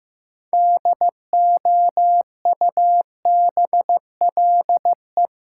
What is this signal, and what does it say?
Morse code, heard as a keyed tone, about 15 wpm, 700 Hz. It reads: DOUBLE
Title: Sentences from Top 1400 Words - New Word Drill Alt - 15wpm